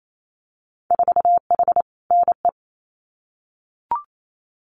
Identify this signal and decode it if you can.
Morse code, heard as a keyed tone, about 28 wpm, 700 Hz. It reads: 4H NE
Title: Winter Field Day Exchanges Encore 28wpm